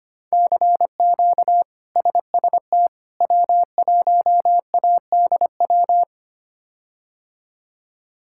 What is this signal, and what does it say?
Morse code, heard as a keyed tone, about 25 wpm, 700 Hz. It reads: CQ SST W1ADW
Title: K1USN SST QSO Practice - 25wpm